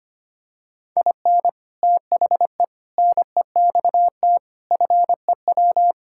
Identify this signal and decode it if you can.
Morse code, heard as a keyed tone, about 25 wpm, 700 Hz. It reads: IN THE NEXT FEW
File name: practice / 25wpm / Sentences with 3-Word Phrases - 25wpm